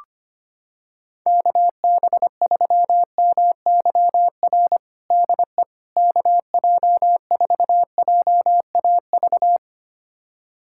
Morse code, heard as a keyed tone, about 25 wpm, 700 Hz. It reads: KB3MYR DE KJ4JAV
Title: USA Callsign DE Callsign - 6 Characters - 25wpm